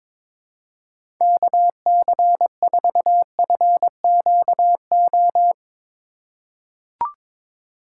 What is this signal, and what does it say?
Morse code, heard as a keyed tone, about 22 wpm, 700 Hz. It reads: KC4FQO
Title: USA Call Signs - 6 Characters - 2x Character Spacing - 22wpm